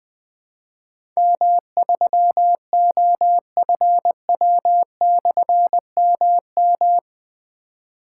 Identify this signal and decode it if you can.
Morse code, heard as a keyed tone, about 21 wpm, 700 Hz. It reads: M3OFW/MM
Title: Difficult Call Signs 20wpm